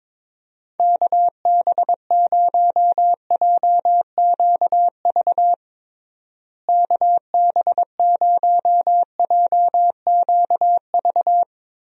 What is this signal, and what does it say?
Morse code, heard as a keyed tone, about 22 wpm, 700 Hz. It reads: KB0JQV KB0JQV
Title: USA Call Signs - 6 Characters - Repeated 3x - 22wpm